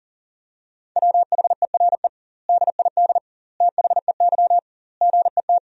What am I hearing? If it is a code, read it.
Morse code, heard as a keyed tone, about 37 wpm, 700 Hz. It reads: WHERE DID THEY GET
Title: 2x Word-Spacing - Sentences from Top 400 Words - 40wpm